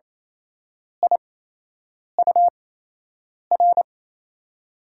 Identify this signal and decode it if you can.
Morse code, heard as a keyed tone, about 28 wpm, spaced as for 8 wpm, 700 Hz. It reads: IUR
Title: USA Call Signs - 6 Characters - 8x Character Spacing - 28wpm